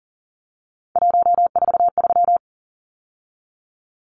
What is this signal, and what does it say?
Morse code, heard as a keyed tone, about 40 wpm, 700 Hz. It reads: 143
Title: Numbers - 3-Digit 40wpm